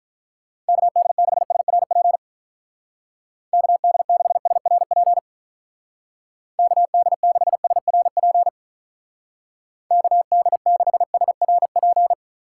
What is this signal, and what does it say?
Morse code, heard as a keyed tone, about 44 wpm, 700 Hz. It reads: KD6SRP KD6SRP KD6SRP KD6SRP